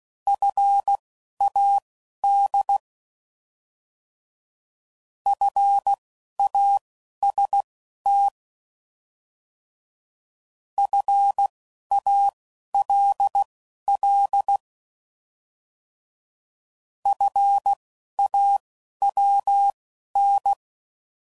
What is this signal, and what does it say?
Morse code, heard as a keyed tone, about 16 wpm, spaced as for 11 wpm, 800 Hz. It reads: FAD FAST FALL FAWN